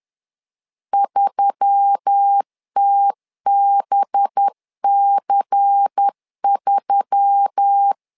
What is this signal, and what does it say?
Morse code, heard as a keyed tone, about 10 wpm, 800 Hz. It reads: 3TBC3
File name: LEZIONI / 5 LBVC / 5 LBVC 10wpm 2349